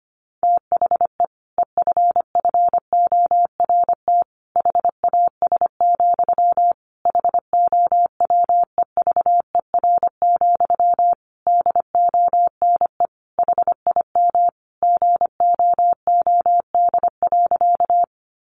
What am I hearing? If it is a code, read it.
Morse code, heard as a keyed tone, about 25 wpm, 700 Hz. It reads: THE EFFORT HAS, HOWEVER, DONE HIM GOOD.
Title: Sentences from Top 1200 Words 25wpm